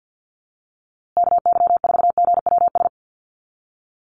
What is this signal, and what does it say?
Morse code, heard as a keyed tone, about 48 wpm, 700 Hz. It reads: KY4GWS
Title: USA Call Signs Repeated 3x 50wpm